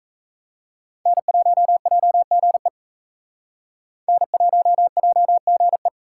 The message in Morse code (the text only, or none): N1JGE N1JGE